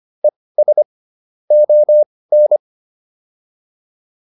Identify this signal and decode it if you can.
Morse code, heard as a keyed tone, about 25 wpm, spaced as for 18 wpm, 600 Hz. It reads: ES ON